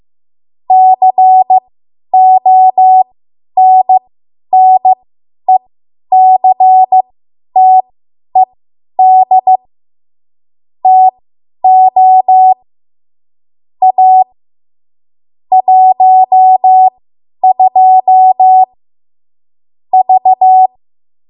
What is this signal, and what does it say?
Morse code, heard as a keyed tone, about 15 wpm, spaced as for 10 wpm, 750 Hz. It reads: CONNECTED TO A 12 V
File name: Archive / 10 WPM / 10WPM